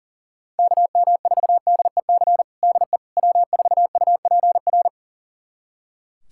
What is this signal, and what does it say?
Morse code, heard as a keyed tone, about 40 wpm, 700 Hz. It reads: KM4DEC DE W4UPR
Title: USA Callsign DE Callsign 40wpm